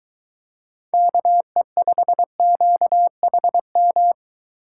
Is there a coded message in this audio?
Yes: Morse code from 0.9 to 4.1 s, heard as a keyed tone, about 22 wpm, 700 Hz. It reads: KE5QHM